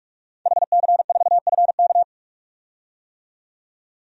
Morse code, heard as a keyed tone, about 41 wpm, 700 Hz. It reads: HC4FK
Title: DXCC-100 Call Signs - 5 Characters - 45wpm